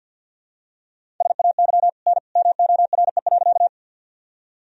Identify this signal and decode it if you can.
Morse code, heard as a keyed tone, about 48 wpm, 700 Hz. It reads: SAY N MORE.
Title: Sentences from Top 200 Words and Abbreviations 50wpm